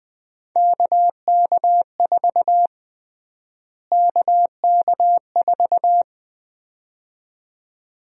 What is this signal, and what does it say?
Morse code, heard as a keyed tone, about 20 wpm, 700 Hz. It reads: KK4 KK4
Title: USA Call Sign Prefixes - Repeated 2x - 20wpm